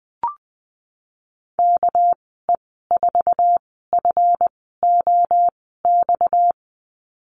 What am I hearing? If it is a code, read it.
Morse code, heard as a keyed tone, about 20 wpm, spaced as for 15 wpm, 700 Hz. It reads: KE4FOX
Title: 238 Lesson 38g - Letter q - Call Signs 20wpm